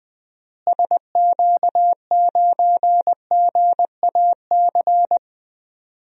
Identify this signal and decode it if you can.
Morse code, heard as a keyed tone, about 20 wpm, 700 Hz. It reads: SQ9GAC
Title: DXCC-100 Call Signs - 6 Characters - Repeated 3x - 20wpm